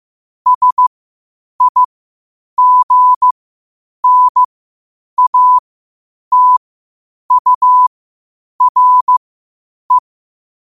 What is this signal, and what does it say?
Morse code, heard as a keyed tone, about 15 wpm, spaced as for 8 wpm, 1000 Hz. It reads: SIGNATURE
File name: qotdfiles / 8wpm